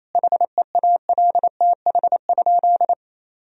Morse code, heard as a keyed tone, about 27 wpm, 700 Hz. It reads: HEALTH?
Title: Sentences from Top 1000 Words - Review Part 3 - 28wpm